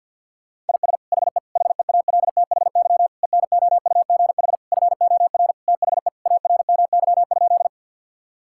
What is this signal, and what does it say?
Morse code, heard as a keyed tone, about 48 wpm, 690 Hz. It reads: IS HE HEALTHY ENOUGH FOR THE ARMY?
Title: Sentences from Top 1000 Words - Review Part 5 - 50wpm